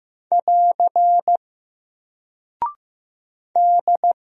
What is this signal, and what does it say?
Morse code, heard as a keyed tone, about 15 wpm, 700 Hz. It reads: <AR> D
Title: Single Character (Letter:Number:Punctuation:Prosign) 15wpm